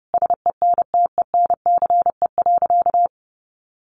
Morse code, heard as a keyed tone, about 30 wpm, 700 Hz. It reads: SENTENCE.
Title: Sentences from Top 1200 Words 30wpm